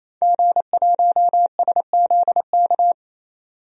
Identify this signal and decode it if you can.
Morse code, heard as a keyed tone, about 28 wpm, 700 Hz. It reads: G1SZK
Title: DXCC-100 Call Signs - 5 Characters - 28wpm